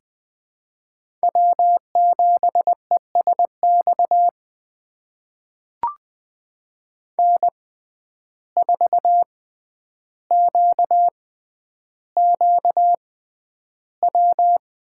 Morse code, heard as a keyed tone, about 20 wpm, spaced as for 10 wpm, 700 Hz. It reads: W7ESX N4QQW
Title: USA Call Signs - 5 Characters - 6x Character Spacing - 20wpm